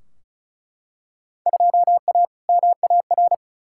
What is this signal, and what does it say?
Morse code, heard as a keyed tone, about 34 wpm, 700 Hz. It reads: 2A MAR
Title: Field Day Exchanges - 35wpm